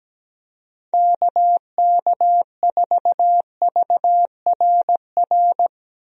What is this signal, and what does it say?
Morse code, heard as a keyed tone, about 17 wpm, 700 Hz. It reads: KK4VRR
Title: USA Call Signs - 6 Characters - 17wpm